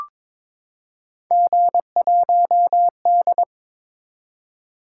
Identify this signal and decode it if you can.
Morse code, heard as a keyed tone, about 22 wpm, 700 Hz. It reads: G1D
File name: practice / 22wpm / Three Letter-Number Combinations - 22wpm